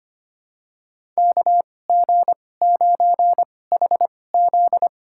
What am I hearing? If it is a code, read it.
Morse code, heard as a keyed tone, about 25 wpm, spaced as for 18 wpm, 700 Hz. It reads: KG9HZ
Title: USA Call Signs - 2x Character Spacing - 25wpm